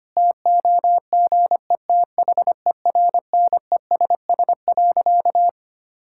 Morse code, heard as a keyed tone, about 25 wpm, 700 Hz. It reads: TOGETHERNESS.
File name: practice / 25wpm / Difficult Sentences with Hints - 25wpm